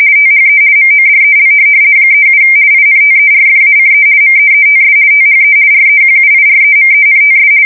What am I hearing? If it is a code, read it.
RTTY, about 50 baud. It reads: THE MODEL 37 INTERFACE MEETS THE REQUIREMENTS
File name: fsk170-50B / teletype-corp